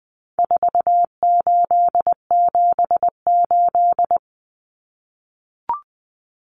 Morse code, heard as a keyed tone, about 20 wpm, 700 Hz. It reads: 4878
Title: Common Cut Numbers - 4-Digit 20wpm